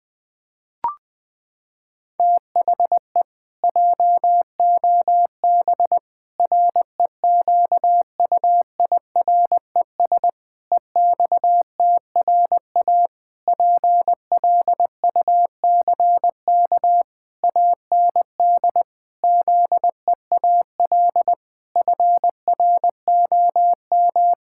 Morse code, heard as a keyed tone, about 20 wpm, 700 Hz. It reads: THE JOB REQUIRES EXTRA PLUCK AND ZEAL FROM
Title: Pangram Challenge v2 - 20wpm